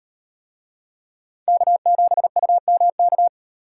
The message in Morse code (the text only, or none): K7UMK